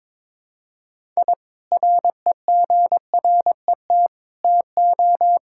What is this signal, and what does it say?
Morse code, heard as a keyed tone, about 22 wpm, 700 Hz. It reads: I REGRET TO
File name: practice / 22wpm / Sentences with 3-Word Phrases - 22wpm